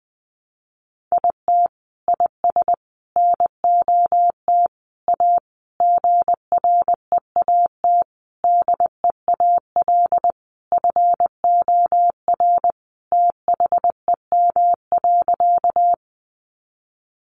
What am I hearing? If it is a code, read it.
Morse code, heard as a keyed tone, about 20 wpm, 700 Hz. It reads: IT IS NOT A GREAT DEAL FOR THEM.
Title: Sentences from Top 800 Words 20wpm